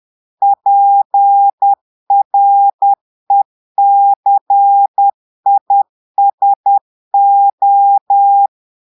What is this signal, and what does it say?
Morse code, heard as a keyed tone, about 10 wpm, 800 Hz. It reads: PRECISO
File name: portugais texte en clair 2 vitesse 10 mots minutes